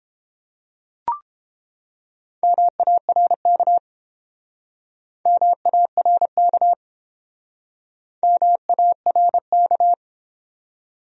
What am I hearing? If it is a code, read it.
Morse code, heard as a keyed tone, about 30 wpm, 700 Hz. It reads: MARK MARK MARK